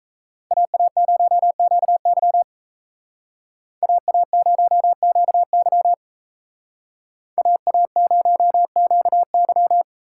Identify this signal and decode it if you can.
Morse code, heard as a keyed tone, about 35 wpm, 700 Hz. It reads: AA0QY AA0QY AA0QY
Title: Speed Racing - USA 2x2 Call Signs 28wpm